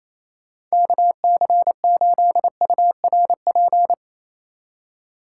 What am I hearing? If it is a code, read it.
Morse code, heard as a keyed tone, about 36 wpm, 700 Hz. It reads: KC8URP